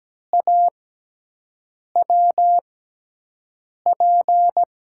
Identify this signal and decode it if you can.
Morse code, heard as a keyed tone, about 17 wpm, spaced as for 6 wpm, 700 Hz. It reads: AWP
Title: USA Call Signs - 6x Character Spacing - 17wpm